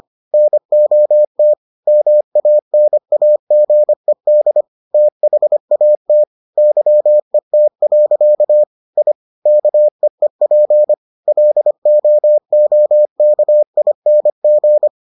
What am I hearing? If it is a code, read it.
Morse code, heard as a keyed tone, about 25 wpm, 600 Hz. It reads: NOT MANAGED THAT YET. I KEEP LOOKING